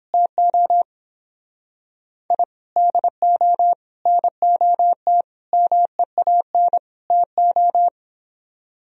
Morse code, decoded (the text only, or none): TO I DO NOT MEAN TO